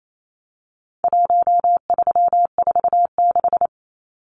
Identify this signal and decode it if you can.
Morse code, heard as a keyed tone, about 28 wpm, 700 Hz. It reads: 1346
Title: Numbers - Contest Style - 4-Digit 28wpm